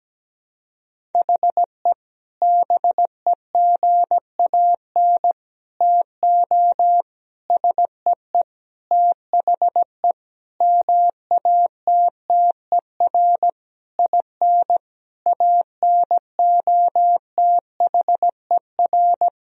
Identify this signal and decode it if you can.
Morse code, heard as a keyed tone, about 17 wpm, 700 Hz. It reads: HE BEGAN TO SEE THE MATTER IN ANOTHER
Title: Sentences from Top 1000 Words - Review Part 1 - 17wpm